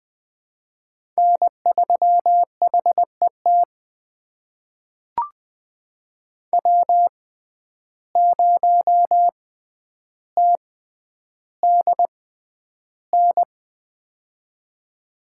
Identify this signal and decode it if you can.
Morse code, heard as a keyed tone, about 20 wpm, spaced as for 10 wpm, 700 Hz. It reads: N3HET W0TDN
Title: USA Call Signs - 5 Characters - 6x Character Spacing - 20wpm